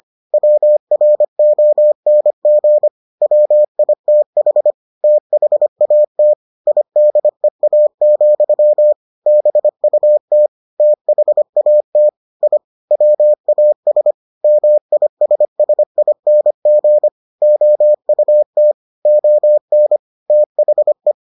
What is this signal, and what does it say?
Morse code, heard as a keyed tone, about 25 wpm, 600 Hz. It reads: WRONG WITH THAT IDEA, BUT THAT I WAS MISSING OUT ON THE